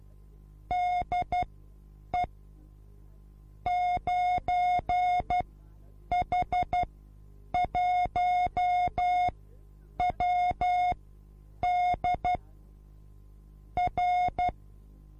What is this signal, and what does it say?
Morse code, heard as a keyed tone, about 12 wpm, spaced as for 8 wpm, 720 Hz. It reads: DE 9H1WD R
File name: repos / Morsekurs / DJ4UF Morsen24